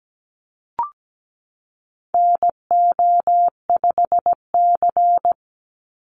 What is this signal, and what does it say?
Morse code, heard as a keyed tone, about 17 wpm, 700 Hz. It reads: NO5C
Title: USA Call Signs 17wpm